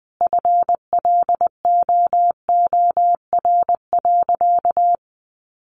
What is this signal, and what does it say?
Morse code, heard as a keyed tone, about 20 wpm, 700 Hz. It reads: FLOOR.
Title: Additional Easy Phrases 20wpm